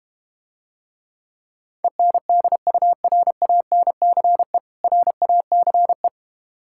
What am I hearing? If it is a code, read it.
Morse code, heard as a keyed tone, about 32 wpm, 700 Hz. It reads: ENDURANCE RACE